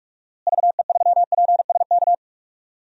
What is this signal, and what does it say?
Morse code, heard as a keyed tone, about 41 wpm, 700 Hz. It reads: VE3PSK
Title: DXCC-100 Call Signs - 6 Characters - Repeated 3x - 45wpm